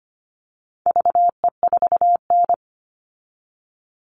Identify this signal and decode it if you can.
Morse code, heard as a keyed tone, about 25 wpm, 700 Hz. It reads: VE4N
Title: All Cut Numbers - 4-Digit 25wpm